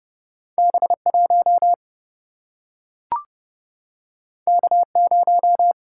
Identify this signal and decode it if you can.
Morse code, heard as a keyed tone, about 29 wpm, 700 Hz. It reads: B1 K0
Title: Single-Letter Single-Number 30wpm